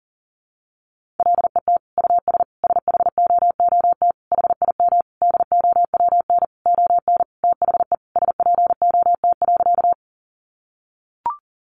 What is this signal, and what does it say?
Morse code, heard as a keyed tone, about 40 wpm, 700 Hz. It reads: LET US SHOOT HIM DOWN ON THE SPOT.
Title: Sentences from Top 1200 Words 40wpm